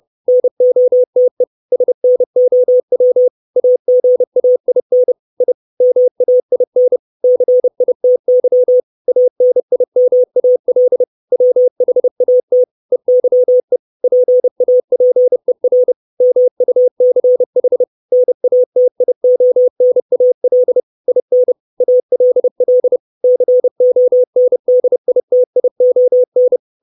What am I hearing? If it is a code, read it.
Morse code, heard as a keyed tone, about 30 wpm, 500 Hz. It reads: NOTE SNOW AGAIN I MAIN CITY ANIMAL WHAT EYE PAPER MUCH NATIONAL IN ALL CONDITION
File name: 632 Words 30WPM